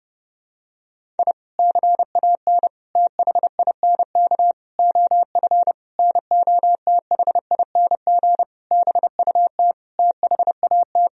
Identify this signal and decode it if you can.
Morse code, heard as a keyed tone, about 30 wpm, 700 Hz. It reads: I CAN THINK OF NOTHING BUT THAT
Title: Sentences from Top 1000 Words - Review Part 1 - 30wpm